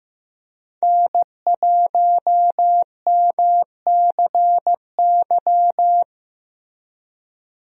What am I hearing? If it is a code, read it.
Morse code, heard as a keyed tone, about 15 wpm, 700 Hz. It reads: N1MCY
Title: USA Call Signs - 5 Characters - Repeated 3x - 15wpm